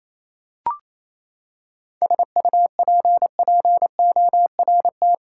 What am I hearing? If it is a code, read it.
Morse code, heard as a keyed tone, about 28 wpm, 700 Hz. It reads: SUPPORT
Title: Seven Letter Words - 28wpm